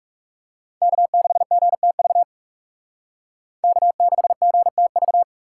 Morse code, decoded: K6GTV K6GTV

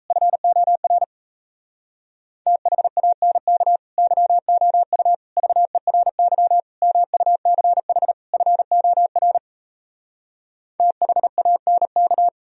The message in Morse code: FOR THANK YOU VERY MUCH FOR THANK

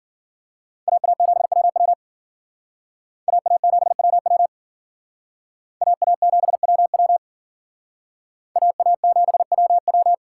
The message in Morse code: AA7WW AA7WW AA7WW AA7WW

